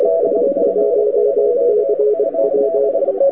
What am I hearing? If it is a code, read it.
Morse code, heard as a keyed tone, about 23 wpm, 420 Hz. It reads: ES1CW
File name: wwcw08 / k5zd wwcw08 1300z